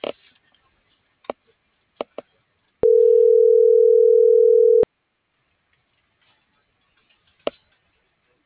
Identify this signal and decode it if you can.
Ringback tone (North America), 440 + 480 Hz.